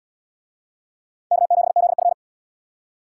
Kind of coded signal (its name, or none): Morse code